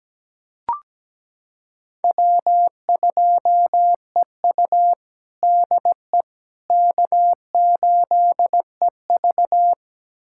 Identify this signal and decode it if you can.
Morse code, heard as a keyed tone, about 17 wpm, 700 Hz. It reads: W2EU DE K8EV
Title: USA Callsign DE Callsign - 4 Characters - 17wpm